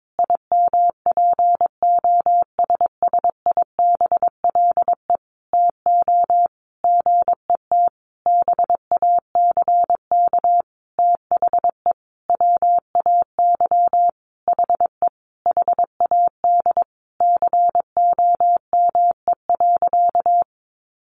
Morse code, heard as a keyed tone, about 22 wpm, 700 Hz. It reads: IMPOSSIBLE TO GET BACK THE WAY HE HAD COME.
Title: Sentences from Top 1100 Words 22wpm